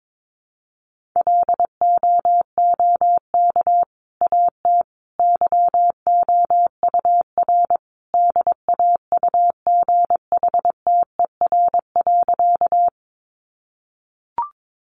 Morse code, heard as a keyed tone, about 22 wpm, 700 Hz. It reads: LOOK AT YOUR DAUGHTER.
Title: Sentences from Top 1500 Words 22wpm